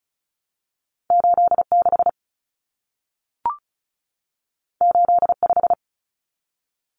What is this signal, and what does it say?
Morse code, heard as a keyed tone, about 34 wpm, 700 Hz. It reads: 86 85
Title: Numbers - 2-Digit 35wpm